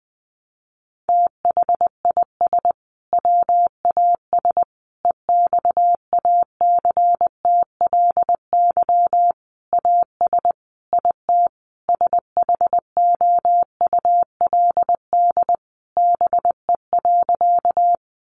Morse code, heard as a keyed tone, about 20 wpm, 700 Hz. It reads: THIS WAS EXACTLY AS IT SHOULD BE.
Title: Sentences from Top 900 Words 20wpm